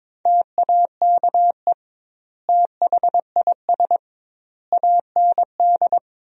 Morse code, heard as a keyed tone, about 22 wpm, 700 Hz. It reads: TAKE THIS AND
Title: 2x Word-Spacing - Sentences from Top 200 Words - 22wpm